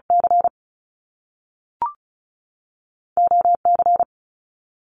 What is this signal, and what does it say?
Morse code, heard as a keyed tone, about 36 wpm, 700 Hz. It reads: C O C
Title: Top Two Letter Combinations in Words 35wpm